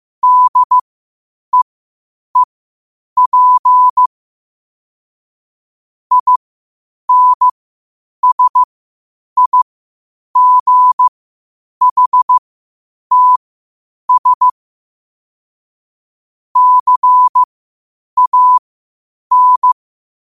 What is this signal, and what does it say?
Morse code, heard as a keyed tone, about 15 wpm, spaced as for 8 wpm, 1000 Hz. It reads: DEEP INSIGHTS CAN
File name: qotdfiles / 8wpm